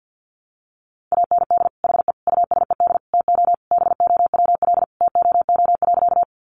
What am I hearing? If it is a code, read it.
Morse code, heard as a keyed tone, about 50 wpm, 700 Hz. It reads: AND HE USED TO BOWL TOO.
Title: Sentences from Top 1200 Words 50wpm